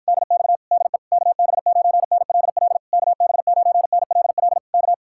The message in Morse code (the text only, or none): DX DE KB9NLL KB9NLL K